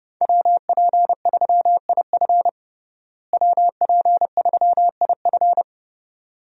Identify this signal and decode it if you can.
Morse code, heard as a keyed tone, about 30 wpm, 700 Hz. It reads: WP3IF WP3IF
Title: USA Call Signs - 5 Characters - Repeated 2x - 30wpm